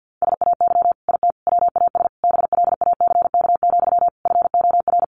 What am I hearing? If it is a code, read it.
Morse code, heard as a keyed tone, about 50 wpm, 700 Hz. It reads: SAY IT WAS BLACK, FOR
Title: Sentences from Top 800 Words 50wpm